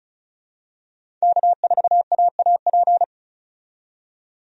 Morse code, heard as a keyed tone, about 34 wpm, 700 Hz. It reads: K4AAP
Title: USA Call Signs - 5 Characters - 35wpm